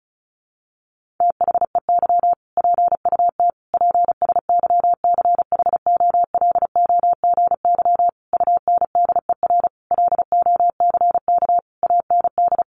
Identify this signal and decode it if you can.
Morse code, heard as a keyed tone, about 34 wpm, 700 Hz. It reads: THEY PUT PSYCHOLOGY UNDER LOCK AND
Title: Sentences from Top 1400 Words 35wpm